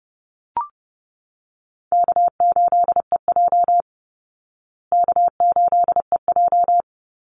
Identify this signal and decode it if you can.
Morse code, heard as a keyed tone, about 30 wpm, 700 Hz. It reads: K8EJ K8EJ
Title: Easy Call Signs Repeated 2x - 30wpm